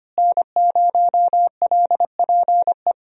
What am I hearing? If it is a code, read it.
Morse code, heard as a keyed tone, about 25 wpm, 700 Hz. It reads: N0LPE
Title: USA Call Signs - 5 Characters - Repeated 2x - 25wpm